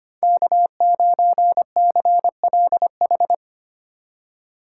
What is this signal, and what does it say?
Morse code, heard as a keyed tone, about 28 wpm, 700 Hz. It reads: K9CLH
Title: Speed Racing - USA Call Signs 25wpm